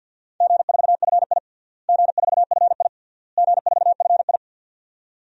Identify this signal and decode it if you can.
Morse code, heard as a keyed tone, about 53 wpm, 700 Hz. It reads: G4FI G4FI G4FI
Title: DXCC-100 Call Signs - 4 Characters - Repeated 3x - 50wpm